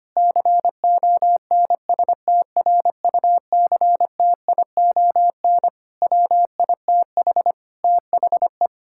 Morse code, heard as a keyed tone, about 25 wpm, 700 Hz. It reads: CONSTRUCTION WITH THE
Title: Sentences from Top 1600 Words 25wpm